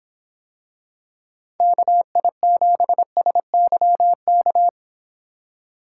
Morse code, heard as a keyed tone, about 24 wpm, 700 Hz. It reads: KI7SYK